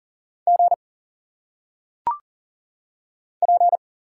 Morse code, heard as a keyed tone, about 40 wpm, 700 Hz. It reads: G P
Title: Single Letters - Phonetic Pronunciation 40wpm